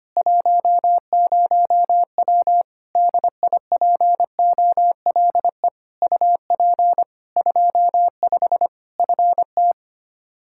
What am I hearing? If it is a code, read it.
Morse code, heard as a keyed tone, about 25 wpm, 700 Hz. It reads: 10W DIPOLE UP 25 FT
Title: K1USN SST QSO Practice 25wpm